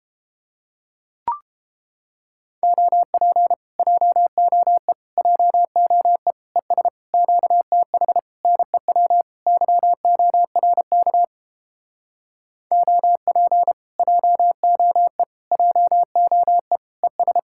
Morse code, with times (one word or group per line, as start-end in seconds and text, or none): 2.6-3.5 OP
3.8-4.9 JOE
5.2-6.3 JOE
6.6-6.9 ES
7.1-8.2 QTH
8.4-9.2 NEW
9.5-11.2 YORK
12.7-13.7 OP
14.0-15.2 JOE
15.5-16.8 JOE
17.0-17.4 ES